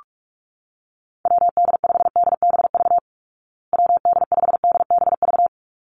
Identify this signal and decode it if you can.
Morse code, heard as a keyed tone, about 44 wpm, 700 Hz. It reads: WD5DBV WD5DBV
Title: Call Signs Repeated 2x - 45wpm